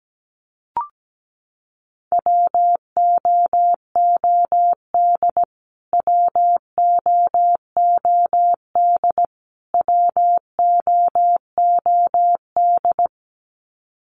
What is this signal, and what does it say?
Morse code, heard as a keyed tone, about 17 wpm, 700 Hz. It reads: WOOD WOOD WOOD
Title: Top 500 Words Repeated 17wpm